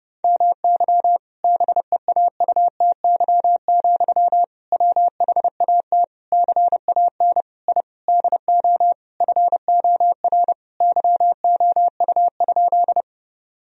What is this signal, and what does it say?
Morse code, heard as a keyed tone, about 32 wpm, 700 Hz. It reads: MY BEAUTY, WHAT CAN I DO FOR YOU?
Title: Sentences from Top 1000 Words - Review Part 2 - 30wpm